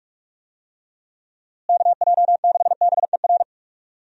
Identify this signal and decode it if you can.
Morse code, heard as a keyed tone, about 43 wpm, 700 Hz. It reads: KJ6BER
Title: USA Call Signs - 6 Characters - Repeated 3x - 45wpm